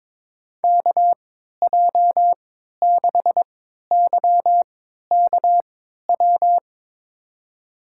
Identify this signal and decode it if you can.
Morse code, heard as a keyed tone, about 22 wpm, spaced as for 12 wpm, 700 Hz. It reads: KJ6YKW
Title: USA Call Signs - 3x Character Spacing - 22wpm